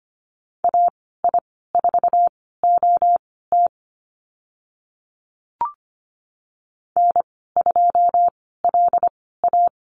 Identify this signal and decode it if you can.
Morse code, heard as a keyed tone, about 25 wpm, spaced as for 16 wpm, 700 Hz. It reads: AI4OT N2LA
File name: Lesson 15g - Number 5 - Call Signs 25wpm